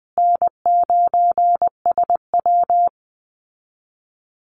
Morse code, heard as a keyed tone, about 20 wpm, 700 Hz. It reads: N9SW
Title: Easy Call Signs Repeated 3x - 20wpm